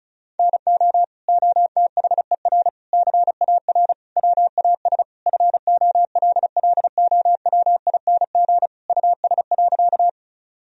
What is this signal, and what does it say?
Morse code, heard as a keyed tone, about 36 wpm, 700 Hz. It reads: NO OTHER CAR WAS FOLLOWING US.
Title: Sentences from Top 1000 Words - Review Part 4 - 35wpm